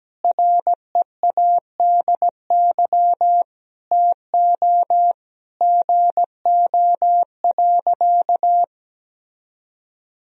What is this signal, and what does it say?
Morse code, heard as a keyed tone, about 17 wpm, 700 Hz. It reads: READY TO GO.
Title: Sentences from Top 1000 Words - Review Part 2 - 17wpm